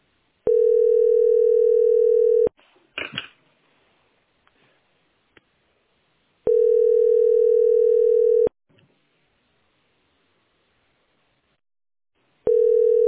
Ringback tone (North America), 440 + 480 Hz.